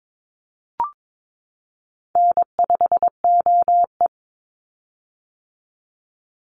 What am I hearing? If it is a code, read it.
Morse code, heard as a keyed tone, about 22 wpm, 700 Hz. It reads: N5OE
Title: Easy Call Signs - 22wpm